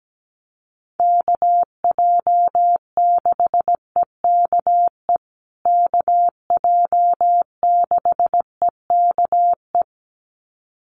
Morse code, heard as a keyed tone, about 17 wpm, 700 Hz. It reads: KJ6EKE KJ6EKE
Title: USA Call Signs Repeated 2x 17wpm